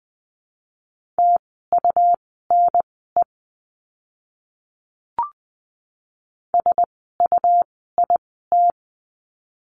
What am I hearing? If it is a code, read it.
Morse code, heard as a keyed tone, about 20 wpm, spaced as for 15 wpm, 700 Hz. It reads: TUNE SUIT
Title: 101 Lesson 17f - Letter u - Words 20wpm